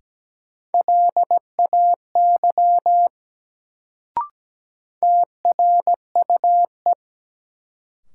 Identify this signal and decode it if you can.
Morse code, heard as a keyed tone, about 17 wpm, 700 Hz. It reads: LAY TRUE
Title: Sentences from Top 400 Words - New Word Drill - 17wpm